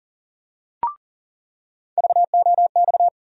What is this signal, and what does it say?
Morse code, heard as a keyed tone, about 40 wpm, 700 Hz. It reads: VOX
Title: USA Call Sign Postfixes - 3 Letters 40wpm